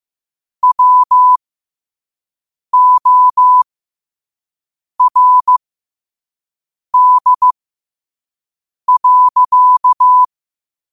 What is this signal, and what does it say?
Morse code, heard as a keyed tone, about 15 wpm, spaced as for 5 wpm, 1000 Hz. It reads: WORD.
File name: qotdfiles / 5wpm